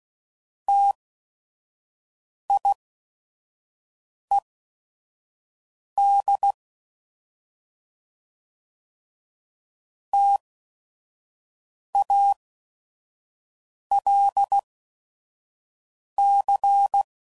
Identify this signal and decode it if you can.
Morse code, heard as a keyed tone, about 16 wpm, spaced as for 5 wpm, 800 Hz. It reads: TIED TALC